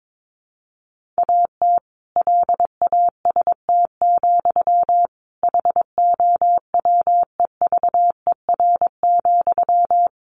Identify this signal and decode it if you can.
Morse code, heard as a keyed tone, about 22 wpm, 700 Hz. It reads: AT LAST, HOWEVER,
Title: Sentences from Top 1500 Words 22wpm